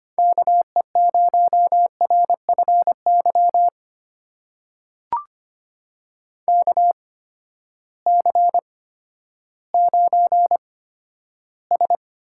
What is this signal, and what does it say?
Morse code, heard as a keyed tone, about 25 wpm, spaced as for 12 wpm, 700 Hz. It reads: KE0RFY KC9S
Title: USA Call Signs - 6 Characters - 8x Character Spacing - 25wpm